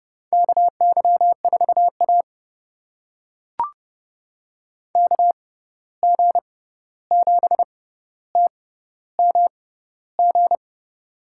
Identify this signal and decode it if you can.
Morse code, heard as a keyed tone, about 32 wpm, spaced as for 14 wpm, 700 Hz. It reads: KY4A KG7TMG
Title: USA Call Signs - 6x Character Spacing - 30wpm